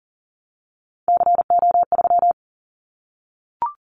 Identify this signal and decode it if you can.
Morse code, heard as a keyed tone, about 40 wpm, 700 Hz. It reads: CO3
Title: Callsign Prefixes for Top 100 DXCC - 40wpm